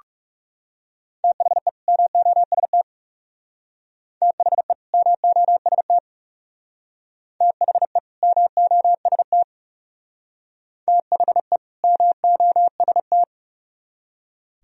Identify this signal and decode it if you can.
Morse code, heard as a keyed tone, about 38 wpm, 700 Hz. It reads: THE MOST THE MOST THE MOST THE MOST
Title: Speed-Racing - Sets of 2 Words 30wpm